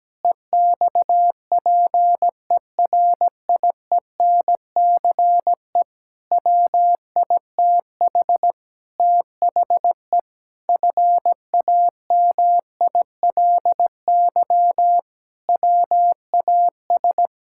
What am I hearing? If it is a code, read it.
Morse code, heard as a keyed tone, about 17 wpm, 700 Hz. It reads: EXPERIENCE WITH THE FAMILY WAS
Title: Sentences from Top 1000 Words - Review Part 6 - 17wpm